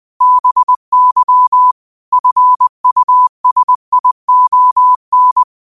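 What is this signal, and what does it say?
Morse code, heard as a keyed tone, about 20 wpm, 1000 Hz. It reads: BY FUSION
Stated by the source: Morse code practice at 20 wpm: famous quotations sent in telegraphy, from URVAG.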